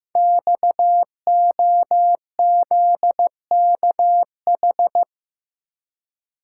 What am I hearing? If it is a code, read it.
Morse code, heard as a keyed tone, about 15 wpm, 700 Hz. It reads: XOZKH
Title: Code Groups - 5 Letter:Numbers - 15wpm